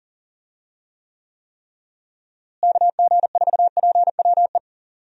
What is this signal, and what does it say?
Morse code, heard as a keyed tone, about 40 wpm, 700 Hz. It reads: KG4PWE
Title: USA Call Signs - 6 Characters - Repeated 3x - 40wpm